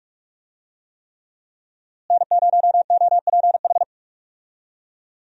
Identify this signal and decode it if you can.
Morse code, heard as a keyed tone, about 46 wpm, 700 Hz. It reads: N0OPH
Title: USA Call Signs - 5 Characters - Repeated 3x - 45wpm